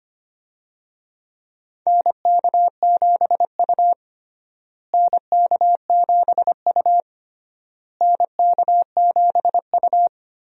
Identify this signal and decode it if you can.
Morse code, heard as a keyed tone, about 25 wpm, 700 Hz. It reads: NK7U NK7U NK7U
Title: USA Call Signs - 4 Characters - Repeated 3x - 25wpm